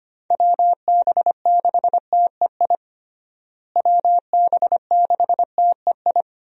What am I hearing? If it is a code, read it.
Morse code, heard as a keyed tone, about 25 wpm, 700 Hz. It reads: WB6TEI WB6TEI
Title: USA Call Signs - 6 Characters - Repeated 2x - 25wpm